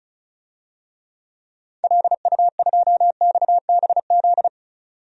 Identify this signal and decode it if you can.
Morse code, heard as a keyed tone, about 36 wpm, 700 Hz. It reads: LU2XBZ
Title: DXCC-100 Call Signs - 6 Characters - 35wpm